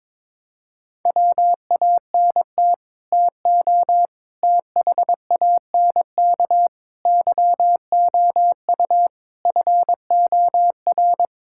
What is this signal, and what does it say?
Morse code, heard as a keyed tone, about 22 wpm, 700 Hz. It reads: WANT TO THANK YOU FOR
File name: practice / 22wpm / Sets of 8 Words - 22wpm